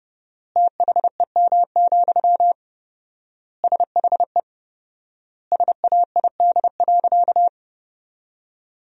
Morse code, heard as a keyed tone, about 30 wpm, 700 Hz. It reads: THEM, SHE SAID.
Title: 4x Word-Spacing - Sentences from Top 400 Words - 30wpm